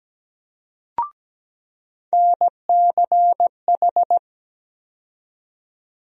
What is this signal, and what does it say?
Morse code, heard as a keyed tone, about 17 wpm, 700 Hz. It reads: NCH
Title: USA Call Sign Postfixes - 3 Letters 17wpm